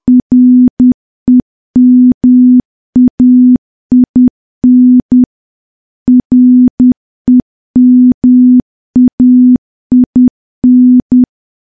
Morse code, heard as a keyed tone, about 10 wpm, 250 Hz. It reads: REMAIN REMAIN
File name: morse / morse-training-10WPM